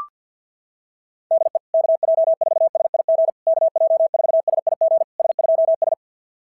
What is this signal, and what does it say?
Morse code, heard as a keyed tone, about 46 wpm, 660 Hz. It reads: DE KJ4SIG KJ4SIG S2S
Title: SOTA QSO Practice 50wpm